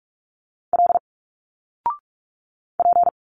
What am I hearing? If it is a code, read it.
Morse code, heard as a keyed tone, about 45 wpm, 700 Hz. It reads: L P
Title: Single Letters 45wpm